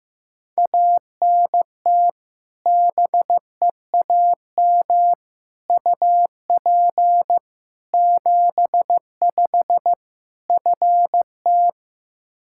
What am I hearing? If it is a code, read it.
Morse code, heard as a keyed tone, about 15 wpm, 700 Hz. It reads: ANT BEAM UP 75 FT
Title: Practice QSO - Exchange 2 - Info - 15wpm